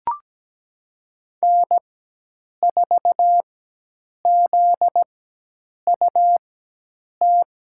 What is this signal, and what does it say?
Morse code, heard as a keyed tone, about 17 wpm, spaced as for 8 wpm, 700 Hz. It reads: N4ZUT
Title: USA Call Signs - 5 Characters - 4x Character Spacing - 17wpm